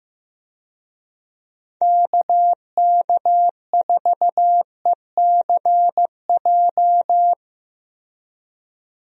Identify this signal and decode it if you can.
Morse code, heard as a keyed tone, about 15 wpm, 700 Hz. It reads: KK4ECJ
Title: USA Call Signs - 6 Characters - Repeated 3x - 15wpm